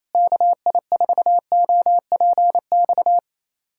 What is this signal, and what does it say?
Morse code, heard as a keyed tone, about 28 wpm, 700 Hz. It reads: KI4OPX
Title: USA Callsign DE Callsign 28wpm